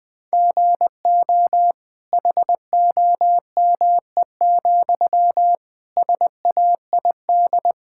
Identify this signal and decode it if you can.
Morse code, heard as a keyed tone, about 21 wpm, 700 Hz. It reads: GO HOME, SAID
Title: Sentences from Top 1600 Words 20wpm